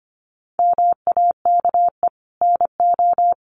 Morse code, heard as a keyed tone, about 25 wpm, 700 Hz. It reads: MAKE NO
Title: Sentences from Top 1500 Words 25wpm